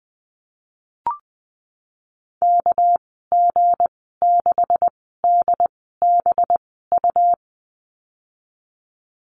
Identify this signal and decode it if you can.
Morse code, heard as a keyed tone, about 20 wpm, spaced as for 15 wpm, 700 Hz. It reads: KG6DBU
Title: 200 Lesson 32g - Letter k - Call Signs 20wpm